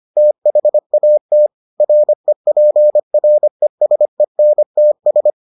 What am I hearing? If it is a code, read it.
Morse code, heard as a keyed tone, about 25 wpm, 600 Hz. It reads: THAT REPRESENTS